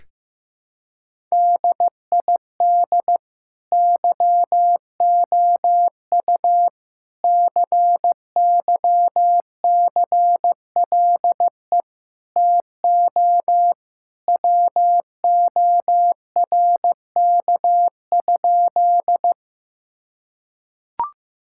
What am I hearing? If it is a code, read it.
Morse code, heard as a keyed tone, about 15 wpm, 700 Hz. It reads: DID YOU CYCLE TO WORK?
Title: Sentences from Top 1000 Words - Review Part 6 - 15wpm